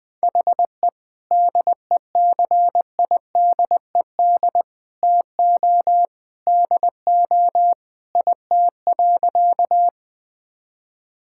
Morse code, heard as a keyed tone, about 21 wpm, 700 Hz. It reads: HE DECIDED TO DO IT.
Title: Sentences with 3-Word Phrases - 20wpm